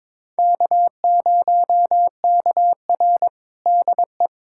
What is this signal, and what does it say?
Morse code, heard as a keyed tone, about 21 wpm, 700 Hz. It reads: K0KR DE
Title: USA Callsign DE Callsign - 4 Characters - 22wpm